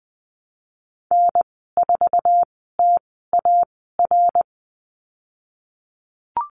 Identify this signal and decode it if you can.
Morse code, heard as a keyed tone, about 20 wpm, spaced as for 15 wpm, 700 Hz. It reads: N4TAR
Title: 055 Lesson 10g - Letter r - Call Signs 20wpm